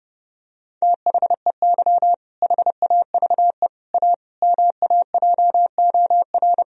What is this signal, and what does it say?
Morse code, heard as a keyed tone, about 30 wpm, 700 Hz. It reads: THEY HAVE A MAJOR
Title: Sentences from Top 1000 Words - Review Part 4 - 30wpm